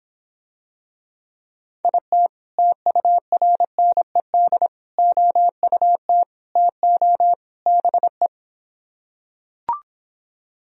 Morse code, heard as a keyed tone, about 26 wpm, 700 Hz. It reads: IT TURNED OUT TO BE